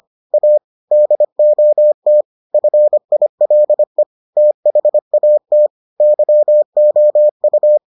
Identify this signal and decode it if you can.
Morse code, heard as a keyed tone, about 25 wpm, 600 Hz. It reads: A DOT FILE THAT YOU